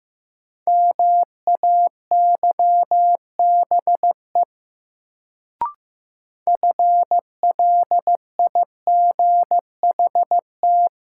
Morse code, heard as a keyed tone, about 15 wpm, 700 Hz. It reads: MAYBE FLIGHT
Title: Sentences from Top 1000 Words - New Word Drill - 15wpm